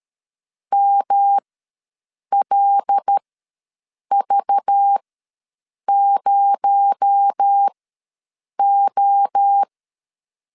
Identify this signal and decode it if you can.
Morse code, heard as a keyed tone, about 13 wpm, spaced as for 7 wpm, 800 Hz. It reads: MLV0O
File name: LEZIONI / 6 PKX0 / 6 PKX0 ws 12wpm 3448